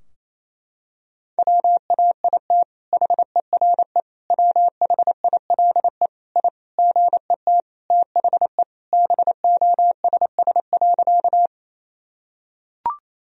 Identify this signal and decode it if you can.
Morse code, heard as a keyed tone, about 28 wpm, 700 Hz. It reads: WAIT HERE WHILE I GET THE BOSS.
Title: Sentences from Top 1000 Words - Review Part 4 - 28wpm